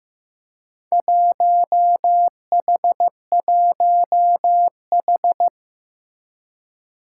Morse code, heard as a keyed tone, about 15 wpm, 700 Hz. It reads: 1H1H
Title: Four Challenging Letter-Number Combinations 15wpm